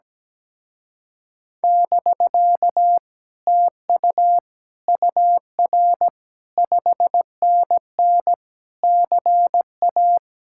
Morse code, heard as a keyed tone, about 17 wpm, 700 Hz. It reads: <BK> TU UR 5NN CA
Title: Speed-Racing Common QSO Phrases - 17wpm